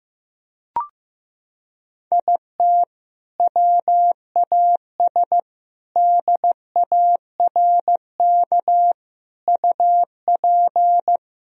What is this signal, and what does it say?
Morse code, heard as a keyed tone, about 15 wpm, 700 Hz. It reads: IT WAS DARK UP
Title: Sentences from Top 1000 Words - Review Part 2 - 15wpm